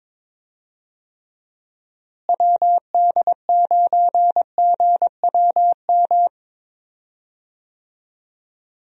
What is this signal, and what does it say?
Morse code, heard as a keyed tone, about 21 wpm, 700 Hz. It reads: WD9GWM
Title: USA Call Signs - 6 Characters - Repeated 2x - 22wpm